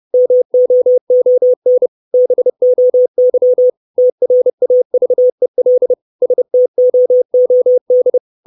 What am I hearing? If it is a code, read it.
Morse code, heard as a keyed tone, about 30 wpm, 500 Hz. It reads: MOON BOY TRAVEL STOOD
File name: 632 Words 30WPM